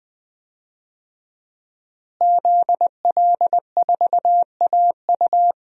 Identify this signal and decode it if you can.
Morse code, heard as a keyed tone, about 20 wpm, 700 Hz. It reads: ZL4AU
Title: DXCC-100 Call Signs - 5 Characters - Repeated 2x - 20wpm